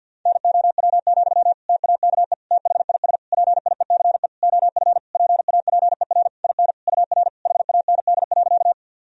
Morse code, heard as a keyed tone, about 51 wpm, 700 Hz. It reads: NOW, TAKE THIS PIECE OF PAPER IN UR HAND.
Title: Sentences from Top 400 Words and Abbreviations 50wpm